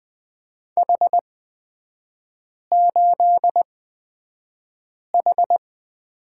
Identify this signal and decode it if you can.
Morse code, heard as a keyed tone, about 20 wpm, spaced as for 5 wpm, 700 Hz. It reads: H8H
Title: 229 Lesson 37e - Letter x - Three Characters 20wpm